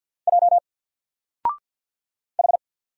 Morse code, heard as a keyed tone, about 50 wpm, 700 Hz. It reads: J H
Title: Single Letter-Number Alt 50wpm